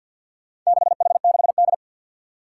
Morse code, heard as a keyed tone, about 52 wpm, 700 Hz. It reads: 6H6D
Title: Four Challenging Letter-Number Combinations 50wpm